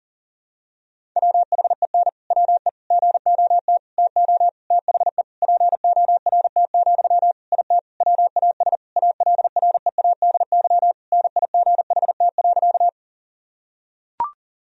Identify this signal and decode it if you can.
Morse code, heard as a keyed tone, about 40 wpm, 700 Hz. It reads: WHEN WE GOT TO THE PORT, IT WAS ALREADY NIGHT.
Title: Sentences from Top 1000 Words - Review Part 1 - 40wpm